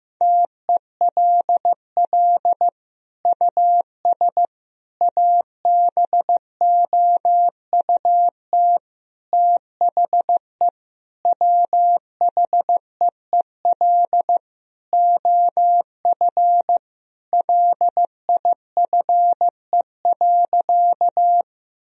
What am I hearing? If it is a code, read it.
Morse code, heard as a keyed tone, about 15 wpm, 700 Hz. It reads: TELL US ABOUT THE WHEEL OF LIFE.
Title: Sentences from Top 1000 Words - Review Part 3 - 15wpm